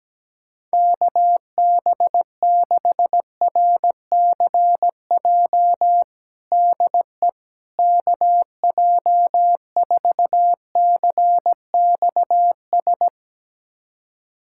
Morse code, heard as a keyed tone, about 17 wpm, 700 Hz. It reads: KB6RCJ DE KJ4CXS
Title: USA Callsign DE Callsign - 6 Characters - 17wpm